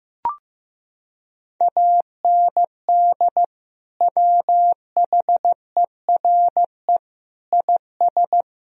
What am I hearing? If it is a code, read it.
Morse code, heard as a keyed tone, about 15 wpm, 700 Hz. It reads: AND WHERE IS
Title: Sentences from Top 1000 Words - Review Part 3 - 15wpm